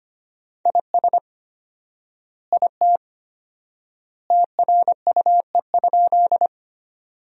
Morse code, heard as a keyed tone, about 25 wpm, 700 Hz. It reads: IS IT TRUE?
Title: 4x Word-Spacing - Sentences from Top 400 Words - 25wpm